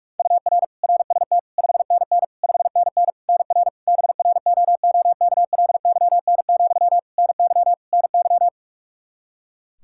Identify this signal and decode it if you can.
Morse code, heard as a keyed tone, about 46 wpm, 700 Hz. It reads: UR RST 5NN 5NN NR BROOKLYN, NY NY
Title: K1USN SST QSO Practice 45wpm